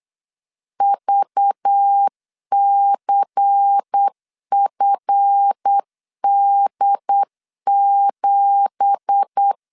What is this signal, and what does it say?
Morse code, heard as a keyed tone, about 8 wpm, 800 Hz. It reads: VCFD7
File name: LEZIONI / 10 SIGNS / 10C DOT 8wpm 3649